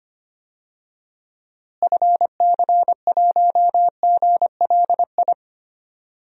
Morse code, heard as a keyed tone, about 25 wpm, 700 Hz. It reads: FC1GLI
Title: DXCC-100 Call Signs - 6 Characters - Repeated 2x - 25wpm